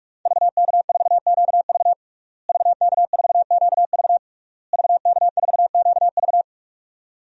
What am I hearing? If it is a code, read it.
Morse code, heard as a keyed tone, about 43 wpm, 700 Hz. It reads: VK4QV VK4QV VK4QV
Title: DXCC-100 Call Signs - 5 Characters - Repeated 3x - 45wpm